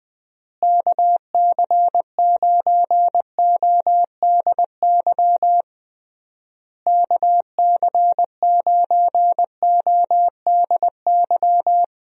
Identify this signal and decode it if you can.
Morse code, heard as a keyed tone, about 20 wpm, 700 Hz. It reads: KC9ODY KC9ODY
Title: USA Call Signs - 6 Characters - Repeated 3x - 20wpm